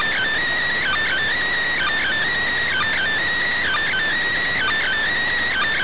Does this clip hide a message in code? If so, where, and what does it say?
Yes: Emergency Alert System (SAME) tones from 0.8 to 4.9 s, heard as harsh buzzing bursts.